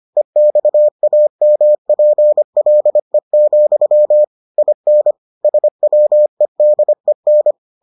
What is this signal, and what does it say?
Morse code, heard as a keyed tone, about 25 wpm, 600 Hz. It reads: EXAMPLE, IN SWEDEN